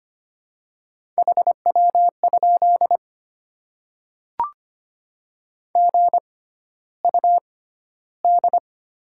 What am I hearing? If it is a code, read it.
Morse code, heard as a keyed tone, about 25 wpm, spaced as for 13 wpm, 700 Hz. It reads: HW? GUD
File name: Top 20 QSO Elements - 6x Character Spacing - 25wpm